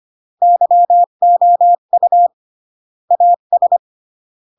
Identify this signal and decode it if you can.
Morse code, heard as a keyed tone, about 25 wpm, 700 Hz. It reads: YOU AS